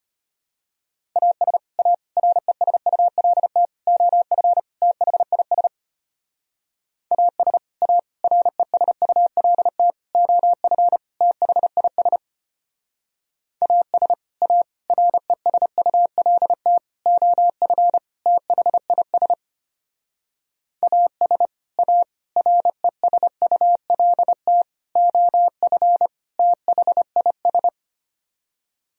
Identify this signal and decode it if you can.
Morse code, heard as a keyed tone, about 32 wpm, 700 Hz. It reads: AS A RESULT OF THIS AS A RESULT OF THIS AS A RESULT OF THIS AS A RESULT OF THIS